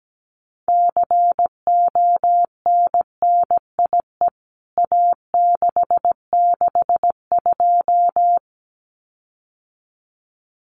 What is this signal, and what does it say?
Morse code, heard as a keyed tone, about 17 wpm, 700 Hz. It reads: CONNIE A662
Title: CWT Exchange 17wpm